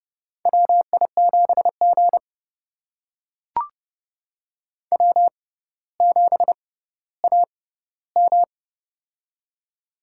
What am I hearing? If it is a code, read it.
Morse code, heard as a keyed tone, about 30 wpm, spaced as for 15 wpm, 700 Hz. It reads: WI7G W7AM
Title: USA Call Signs - 4 Characters - 6x Character Spacing - 30wpm